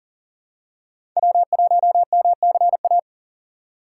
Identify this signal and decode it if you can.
Morse code, heard as a keyed tone, about 37 wpm, 700 Hz. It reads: W1MCA
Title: USA Call Signs - 5 Characters - 4x Character Spacing - 40wpm